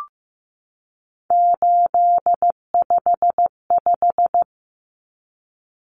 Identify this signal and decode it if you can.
Morse code, heard as a keyed tone, about 15 wpm, 700 Hz. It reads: 855
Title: Numbers - 3-Digit 15wpm